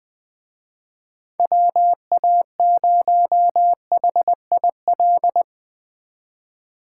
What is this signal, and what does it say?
Morse code, heard as a keyed tone, about 23 wpm, 700 Hz. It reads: WA0HIL
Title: Speed Racing - USA Call Signs 20wpm